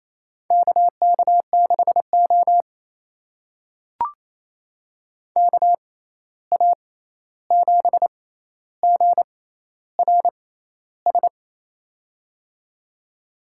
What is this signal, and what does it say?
Morse code, heard as a keyed tone, about 28 wpm, spaced as for 13 wpm, 700 Hz. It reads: KK6O KA7GRS
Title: USA Call Signs - 6x Character Spacing - 28wpm